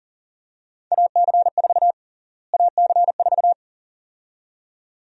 Morse code, heard as a keyed tone, about 37 wpm, 700 Hz. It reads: AC4 AC4
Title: USA Call Sign Prefixes - Repeated 2x - 40wpm